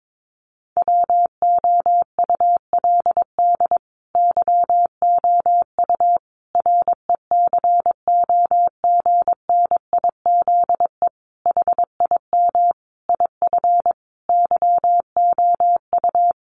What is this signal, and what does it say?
Morse code, heard as a keyed tone, about 22 wpm, 700 Hz. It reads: WOULD YOU RECOGNIZE HIM IF YOU
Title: Sentences from Top 1400 Words 22wpm